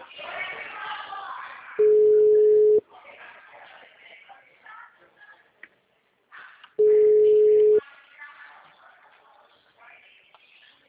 Ringback tone (Europe, 425 Hz).